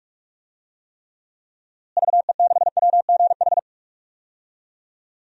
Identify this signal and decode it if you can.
Morse code, heard as a keyed tone, about 41 wpm, 700 Hz. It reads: VE6WGH
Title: DXCC-100 Call Signs - 6 Characters - Repeated 2x - 45wpm